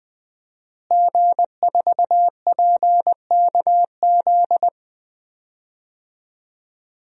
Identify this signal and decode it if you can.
Morse code, heard as a keyed tone, about 20 wpm, 700 Hz. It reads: G4PKZ
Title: DXCC-100 Call Signs - 5 Characters - Repeated 3x - 20wpm